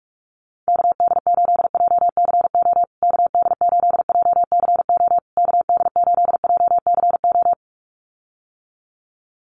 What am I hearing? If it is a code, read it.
Morse code, heard as a keyed tone, about 44 wpm, 690 Hz. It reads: KD8JCO KD8JCO KD8JCO
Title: USA Call Signs Repeated 3x 45wpm